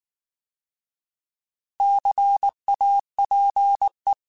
Morse code, heard as a keyed tone, about 19 wpm, 800 Hz. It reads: CAPE
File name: Alphabetical / Alphabetical mp3s / Cs @18